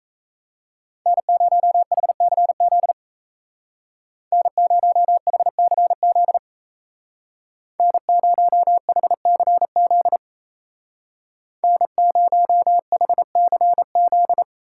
Morse code, heard as a keyed tone, about 34 wpm, 700 Hz. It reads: N0HCZ N0HCZ N0HCZ N0HCZ
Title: Speed Racing - USA 1x3 Call Signs 28wpm